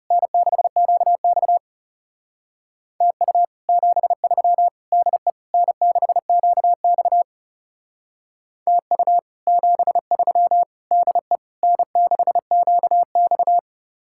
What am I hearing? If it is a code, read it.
Morse code, heard as a keyed tone, about 36 wpm, 700 Hz. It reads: N6QX TU 73 DE N6QX TU 73 DE N6QX
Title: Speed-Racing Common QSO Phrases - 30wpm